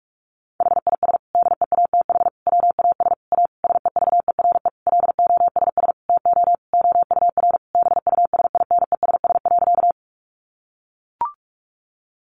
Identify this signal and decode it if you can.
Morse code, heard as a keyed tone, about 44 wpm, 700 Hz. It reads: HIS DEATH WAS A SEVERE LOSS TO OUR BUSINESS.
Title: Sentences from Top 1400 Words 45wpm